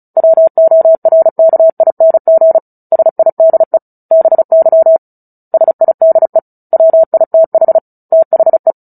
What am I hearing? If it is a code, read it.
Morse code, heard as a keyed tone, about 34 wpm, 650 Hz. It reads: WORKING SIDE BY SIDE WITH THE
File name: Book Two / 09-Chapter 9 - 35 WPM